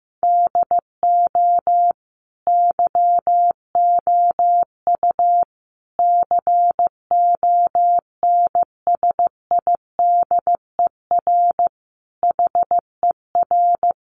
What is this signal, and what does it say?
Morse code, heard as a keyed tone, about 15 wpm, 700 Hz. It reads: DO YOU CONSIDER HER
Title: Sentences from Top 600 Words 15wpm